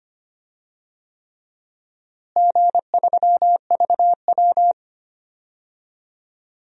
Morse code, heard as a keyed tone, about 25 wpm, 700 Hz. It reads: G3VW
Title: DXCC-100 Call Signs - 4 Characters - Repeated 3x - 25wpm